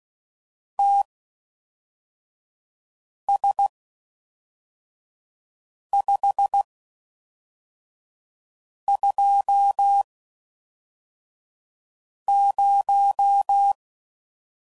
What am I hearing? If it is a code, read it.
Morse code, heard as a keyed tone, about 16 wpm, spaced as for 4 wpm, 800 Hz. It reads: TS520